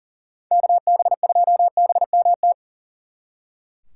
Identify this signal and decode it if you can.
Morse code, heard as a keyed tone, about 40 wpm, 700 Hz. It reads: KB2BMT
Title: Speed Racing - USA 2x3 Call Signs 40wpm